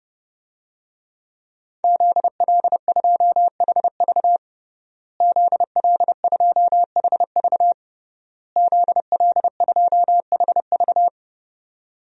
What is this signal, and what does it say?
Morse code, heard as a keyed tone, about 30 wpm, 700 Hz. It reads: ZL2HV ZL2HV ZL2HV
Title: DXCC-100 Call Signs - 5 Characters - Repeated 3x - 30wpm